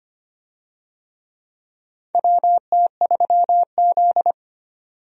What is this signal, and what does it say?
Morse code, heard as a keyed tone, about 25 wpm, 700 Hz. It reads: WT3Z
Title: USA Call Signs - 4 Characters - 25wpm